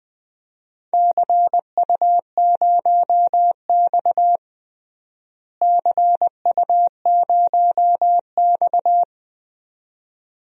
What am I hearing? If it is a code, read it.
Morse code, heard as a keyed tone, about 20 wpm, 700 Hz. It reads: CU0X CU0X
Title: DXCC-100 Call Signs - 4 Characters - Repeated 2x - 20wpm